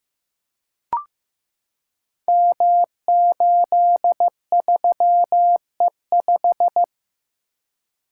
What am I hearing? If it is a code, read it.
Morse code, heard as a keyed tone, about 15 wpm, 700 Hz. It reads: M83E5
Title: Code Groups - 5 Letter:Numbers - 15wpm